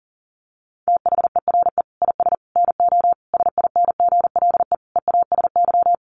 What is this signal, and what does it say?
Morse code, heard as a keyed tone, about 40 wpm, 700 Hz. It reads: THERE IS NO SINGLE EASY
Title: Sentences from Top 1300 Words 40wpm